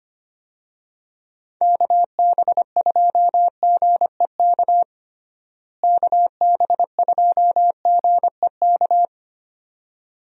Morse code, heard as a keyed tone, about 25 wpm, 700 Hz. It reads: KB2GEK KB2GEK
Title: USA Call Signs - 6 Characters - Repeated 2x - 25wpm